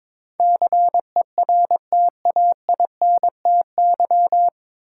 Morse code, heard as a keyed tone, about 22 wpm, 700 Hz. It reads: CERTAINTY
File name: 9 Letter Words - First 3 Letters Given - 22wpm